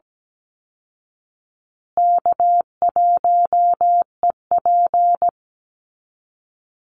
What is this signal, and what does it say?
Morse code, heard as a keyed tone, about 17 wpm, 700 Hz. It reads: K1EP
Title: Easy Call Signs Repeated 3x - 17wpm